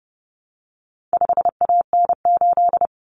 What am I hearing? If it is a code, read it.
Morse code, heard as a keyed tone, about 30 wpm, 700 Hz. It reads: 5AN8
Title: Common Cut Numbers - 4-Digit 30wpm